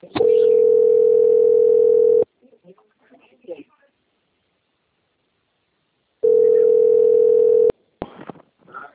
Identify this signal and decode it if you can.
Ringback tone (North America), 440 + 480 Hz, with speech.
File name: out- - - - - .34998